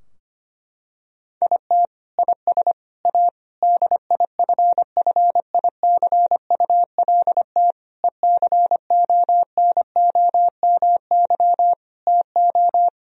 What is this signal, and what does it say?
Morse code, heard as a keyed tone, about 25 wpm, 700 Hz. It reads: IT IS A DIFFICULT ECONOMY TO
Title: Sentences from Top 1000 Words - Review Part 3 - 25wpm